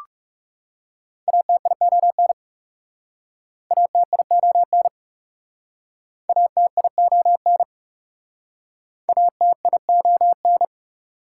Morse code, heard as a keyed tone, about 41 wpm, 700 Hz. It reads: ATION ATION ATION ATION